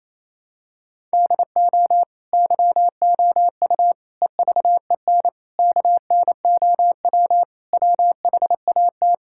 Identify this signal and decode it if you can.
Morse code, heard as a keyed tone, about 28 wpm, 700 Hz. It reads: DO YOU EVEN KNOW WHAT
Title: Sentences from Top 1000 Words - Review Part 5 - 28wpm